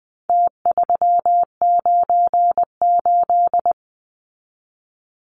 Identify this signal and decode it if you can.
Morse code, heard as a keyed tone, about 20 wpm, 700 Hz. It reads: T398
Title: Numbers - Contest Style - 4-Digit 20wpm